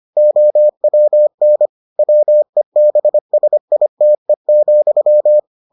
Morse code, heard as a keyed tone, about 25 wpm, 600 Hz. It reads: OWN WEBSITE,